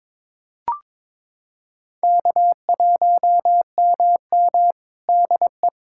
Morse code, heard as a keyed tone, about 22 wpm, 700 Hz. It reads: K1MM DE
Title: USA Callsign DE Callsign - 4 Characters - 22wpm